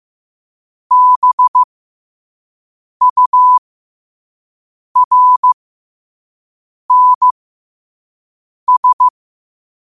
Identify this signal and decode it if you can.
Morse code, heard as a keyed tone, about 15 wpm, spaced as for 5 wpm, 1000 Hz. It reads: BURNS
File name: qotdfiles / 5wpm